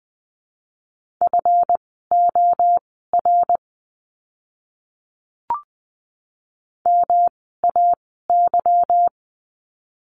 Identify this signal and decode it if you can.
Morse code, heard as a keyed tone, about 20 wpm, spaced as for 14 wpm, 700 Hz. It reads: FOR MAY
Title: 230 Lesson 37f - Letter x - Words 20wpm